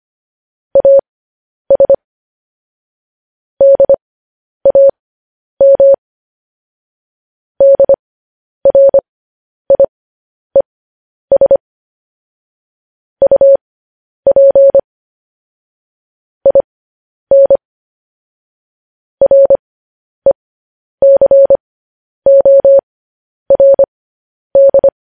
Morse code, heard as a keyed tone, about 25 wpm, spaced as for 10 wpm, 550 Hz. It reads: AS DAM DRIES UP IN RECORD